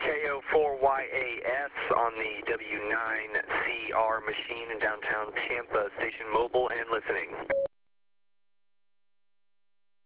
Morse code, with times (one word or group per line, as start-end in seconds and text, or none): none